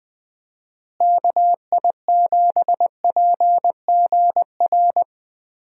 Morse code, heard as a keyed tone, about 20 wpm, 700 Hz. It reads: KI7PGR
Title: USA Call Signs - 6 Characters - Repeated 3x - 20wpm